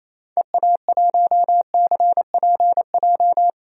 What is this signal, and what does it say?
Morse code, heard as a keyed tone, about 28 wpm, 700 Hz. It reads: EA1CPJ
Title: DXCC-100 Call Signs - 6 Characters - 28wpm